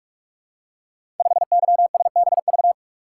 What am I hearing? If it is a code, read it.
Morse code, heard as a keyed tone, about 44 wpm, 700 Hz. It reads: 5YSBV
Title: Code Groups - 5 Letter:Numbers - 45wpm